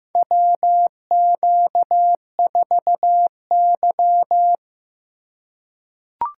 Morse code, heard as a keyed tone, about 15 wpm, 700 Hz. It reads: WQ4Y
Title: USA Call Signs - 4 Characters - 3x Character Spacing - 15wpm